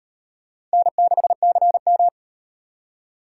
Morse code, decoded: N6CM

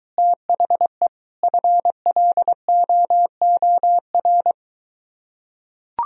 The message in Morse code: THE FLOOR

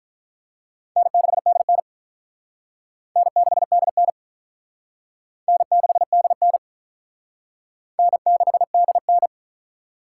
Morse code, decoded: N6DN N6DN N6DN N6DN

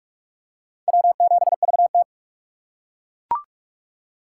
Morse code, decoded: W7VT

